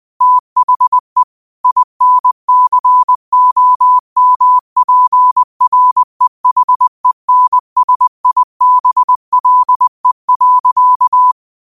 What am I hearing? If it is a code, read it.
Morse code, heard as a keyed tone, about 20 wpm, 1000 Hz. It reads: THE INCOMPREHENSIBLE.
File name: qotdfiles / 20wpm